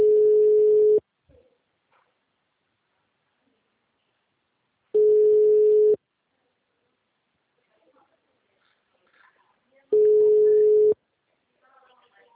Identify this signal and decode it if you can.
Ringback tone (Europe, 425 Hz), with speech.